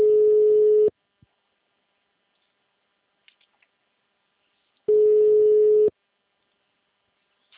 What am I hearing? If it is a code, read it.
Ringback tone (Europe, 425 Hz).
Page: Sumare - SP